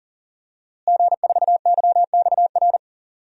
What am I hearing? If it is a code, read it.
Morse code, heard as a keyed tone, about 38 wpm, 700 Hz. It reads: G4YXR
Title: DXCC-100 Call Signs - 5 Characters - Repeated 2x - 40wpm